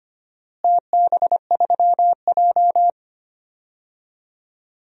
Morse code, heard as a keyed tone, about 25 wpm, 700 Hz. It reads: TB3J
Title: DXCC-100 Call Signs - 4 Characters - Repeated 2x - 25wpm